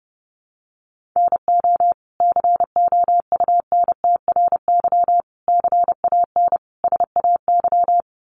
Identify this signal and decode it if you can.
Morse code, heard as a keyed tone, about 30 wpm, 700 Hz. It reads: NO COUNTRY CAN SAY
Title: Sentences from Top 800 Words 30wpm